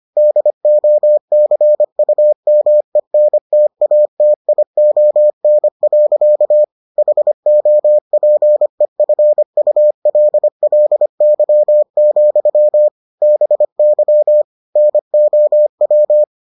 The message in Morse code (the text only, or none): DOCUMENTATION. HOPEFULLY, BY NOW